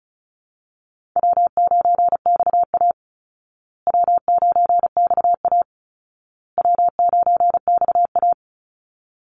Morse code, heard as a keyed tone, about 34 wpm, 700 Hz. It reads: W9XA W9XA W9XA
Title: Call Signs Repeated 2x - 35wpm